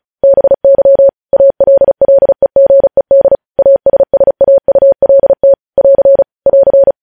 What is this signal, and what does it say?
Morse code, heard as a keyed tone, about 36 wpm, 550 Hz. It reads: BY ALLEGED ASSAULT <AR> <AR>
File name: podcast / News headlines-1x-35x35